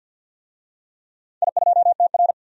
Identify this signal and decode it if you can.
Morse code, heard as a keyed tone, about 49 wpm, 700 Hz. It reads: I2TR
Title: DXCC-100 Call Signs - 4 Characters - 50wpm